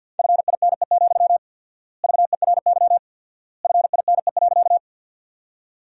Morse code, heard as a keyed tone, about 50 wpm, 700 Hz. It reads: FINE, VERY FINE.
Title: 4x Word-Spacing - Sentences from Top 500 Words - 50wpm